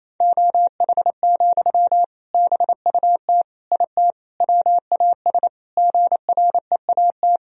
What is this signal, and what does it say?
Morse code, heard as a keyed tone, about 28 wpm, 700 Hz. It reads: OH, BUT IT WAS GREAT
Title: Sentences from Top 1000 Words - Review Part 4 - 28wpm